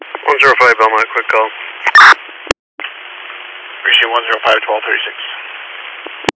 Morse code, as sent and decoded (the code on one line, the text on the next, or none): none